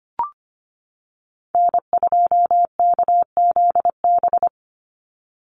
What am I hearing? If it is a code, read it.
Morse code, heard as a keyed tone, about 25 wpm, 690 Hz. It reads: N2KZB
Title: USA Call Signs 25wpm